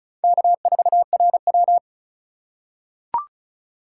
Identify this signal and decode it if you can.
Morse code, heard as a keyed tone, about 36 wpm, 700 Hz. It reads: K4RW
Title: USA Call Signs - 4x Character Spacing - 35wpm